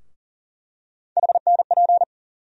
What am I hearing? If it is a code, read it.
Morse code, heard as a keyed tone, about 40 wpm, 700 Hz. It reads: HNP
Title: USA Call Sign Postfixes - 3 Letters 40wpm